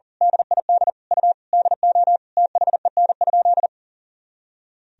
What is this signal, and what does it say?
Morse code, heard as a keyed tone, about 40 wpm, 700 Hz. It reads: DID U DO THEN?
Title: Sentences from Top 100 Words and Abbreviations 40wpm